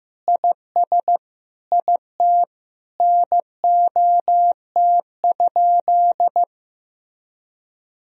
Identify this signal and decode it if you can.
Morse code, heard as a keyed tone, about 15 wpm, 700 Hz. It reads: IS IT NOT?
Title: Sentences from Top 1000 Words - Review Part 4 - 15wpm